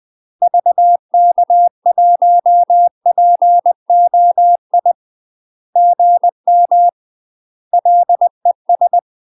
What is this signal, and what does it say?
Morse code, heard as a keyed tone, about 20 wpm, 700 Hz. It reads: VK1POI GM LES
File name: UK MASTER QSOs TRAINING / 20WPM / UK MASTER QSOs 0005